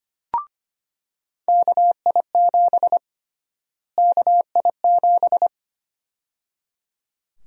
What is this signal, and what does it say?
Morse code, heard as a keyed tone, about 25 wpm, 700 Hz. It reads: KI7 KI7
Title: USA Call Sign Prefixes - Repeated 2x - 25wpm